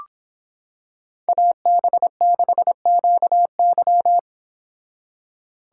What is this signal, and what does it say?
Morse code, heard as a keyed tone, about 26 wpm, 700 Hz. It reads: AB6QY